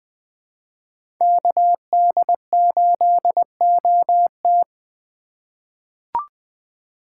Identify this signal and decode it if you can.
Morse code, heard as a keyed tone, about 21 wpm, 700 Hz. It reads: KD8OT
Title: USA Call Signs - 5 Characters - 3x Character Spacing - 20wpm